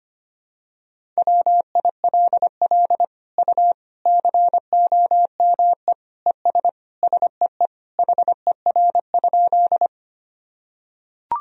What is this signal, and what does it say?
Morse code, heard as a keyed tone, about 25 wpm, 700 Hz. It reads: WILL U COME ES SEE HER?
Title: Sentences from Top 100 Words and Abbreviations 25wpm